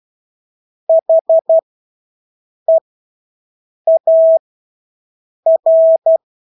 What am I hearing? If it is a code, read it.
Morse code, heard as a keyed tone, about 12 wpm, spaced as for 6 wpm, 650 Hz. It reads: HEAR
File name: Engelske ord 12-6 wpm